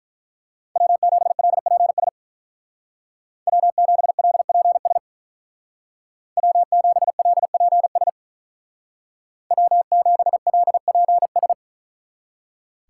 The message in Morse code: W7LPS W7LPS W7LPS W7LPS

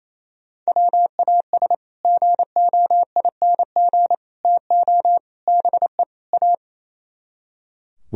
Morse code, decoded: WAS GOING TO BE A

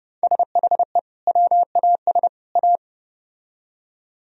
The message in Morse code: SHE WAS A

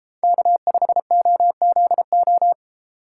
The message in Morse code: K5OZO

